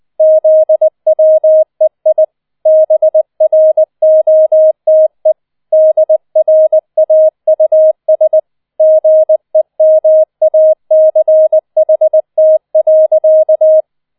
Morse code, heard as a keyed tone, about 20 wpm, 630 Hz. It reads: ZWEI BROTE DRAUS GEMACHT.